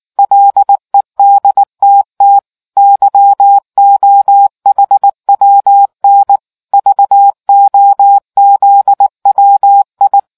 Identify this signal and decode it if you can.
Morse code, heard as a keyed tone, about 19 wpm, 800 Hz. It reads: LEDTT YOHWN VOZWI